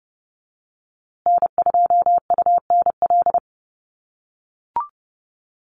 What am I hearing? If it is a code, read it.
Morse code, heard as a keyed tone, about 30 wpm, 700 Hz. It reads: N2UNL
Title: USA Call Signs Repeated 2x 30wpm